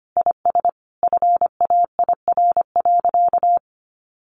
Morse code, heard as a keyed tone, about 25 wpm, 700 Hz. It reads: IS FAIR.
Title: Sentences from Top 1300 Words 25wpm